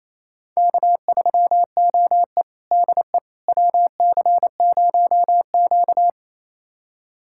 Morse code, heard as a keyed tone, about 28 wpm, 700 Hz. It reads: K3OE DE WC0Q
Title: USA Callsign DE Callsign - 4 Characters - 28wpm